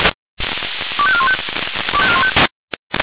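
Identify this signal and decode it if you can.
Five-tone selective call (ZVEI), heard as quick runs of five tones, tones of about 70 ms. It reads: 25415
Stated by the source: ZVEI five-tone selective call 25202, received with an SDR.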